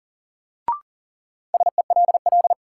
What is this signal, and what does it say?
Morse code, heard as a keyed tone, about 40 wpm, 700 Hz. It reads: SELL
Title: Sentences from Top 1100 Words - New Word Drill Alt - 40wpm